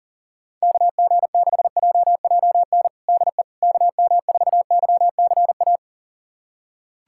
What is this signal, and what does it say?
Morse code, heard as a keyed tone, about 40 wpm, 700 Hz. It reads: KG6JJN DE KM4YCA
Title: USA Callsign DE Callsign - 6 Characters - 40wpm